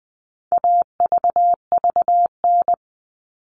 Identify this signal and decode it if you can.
Morse code, heard as a keyed tone, about 20 wpm, 700 Hz. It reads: AVVN
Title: All Cut Numbers - 4-Digit 20wpm